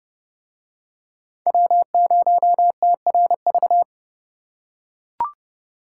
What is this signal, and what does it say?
Morse code, heard as a keyed tone, about 30 wpm, 700 Hz. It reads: W0TRV